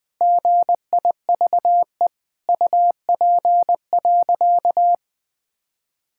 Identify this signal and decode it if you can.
Morse code, heard as a keyed tone, about 20 wpm, 700 Hz. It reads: GIVE UP.
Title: Sentences with 8-Word Phrases - 20wpm